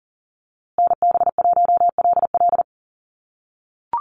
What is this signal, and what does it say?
Morse code, heard as a keyed tone, about 40 wpm, 700 Hz. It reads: NB1LL
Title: USA Call Signs Repeated 3x 40wpm